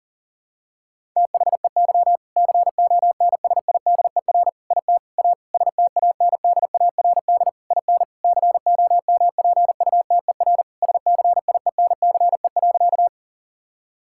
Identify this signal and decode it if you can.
Morse code, heard as a keyed tone, about 37 wpm, 700 Hz. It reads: THEY CONSIDER IT A STANDARD IN COMPUTER SCIENCE.
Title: Sentences from Top 1000 Words - Review Part 6 - 40wpm